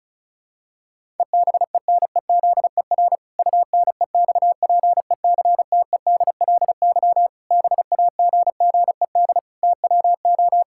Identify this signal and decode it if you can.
Morse code, heard as a keyed tone, about 36 wpm, 700 Hz. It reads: EBENEZER UNEXPECTEDLY BAGGED TWO
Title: Pangram Challenge v2 - 35wpm